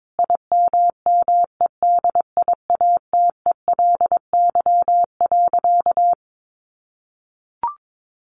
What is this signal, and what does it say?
Morse code, heard as a keyed tone, about 22 wpm, 700 Hz. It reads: IMMEDIATELY.
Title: Sentences from Top 1100 Words 22wpm